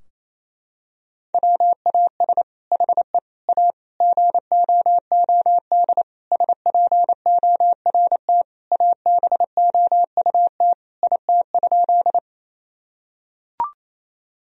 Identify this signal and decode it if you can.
Morse code, heard as a keyed tone, about 28 wpm, 700 Hz. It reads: WAS HE A GOOD SPORT ABOUT IT?
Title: Sentences from Top 1000 Words - Review Part 4 - 28wpm